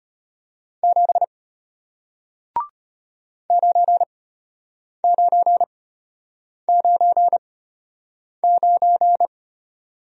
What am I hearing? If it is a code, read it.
Morse code, heard as a keyed tone, about 33 wpm, spaced as for 8 wpm, 700 Hz. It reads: 7 9999